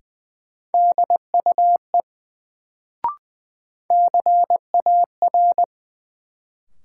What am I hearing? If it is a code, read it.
Morse code, heard as a keyed tone, about 20 wpm, 700 Hz. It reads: DUE CAR
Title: Sets of 5 Words Encore - New Word Drill - 20wpm